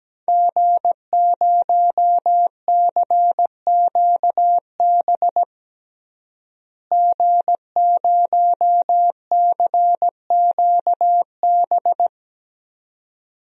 Morse code, heard as a keyed tone, about 17 wpm, 700 Hz. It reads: G0CQB G0CQB
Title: DXCC-100 Call Signs - 5 Characters - Repeated 2x - 17wpm